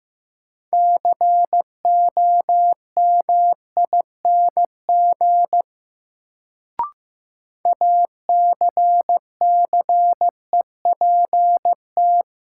Morse code, heard as a keyed tone, about 15 wpm, 700 Hz. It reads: COMING ACCEPT
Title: Sentences from Top 1300 Words - New Word Drill - 15wpm